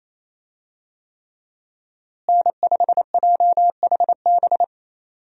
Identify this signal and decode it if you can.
Morse code, heard as a keyed tone, about 27 wpm, 700 Hz. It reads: N5JHB
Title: USA Call Signs - 5 Characters - Repeated 3x - 28wpm